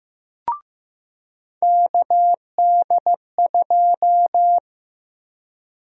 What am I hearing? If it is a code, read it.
Morse code, heard as a keyed tone, about 15 wpm, 700 Hz. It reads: KD2
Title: USA Call Sign Prefixes - Repeated 2x - 15wpm